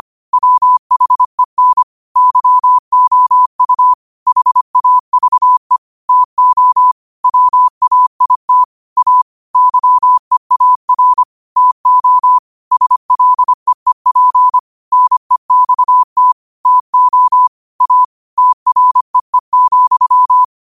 Morse code, heard as a keyed tone, about 25 wpm, 1000 Hz. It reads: WHEN YOU HAVE TO WAIT A YEAR TO SLEEP NEXT TO A TREE,
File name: qotdfiles / 25wpm